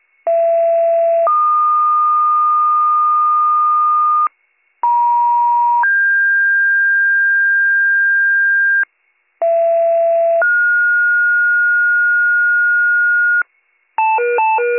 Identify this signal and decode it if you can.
Two-tone page, heard as a one-second tone, then a three-second tone, about 670 then 1220 Hz, about 950 then 1600 Hz, about 670 then 1430 Hz, with noise. It is heard from 0.3 to 13.4 s.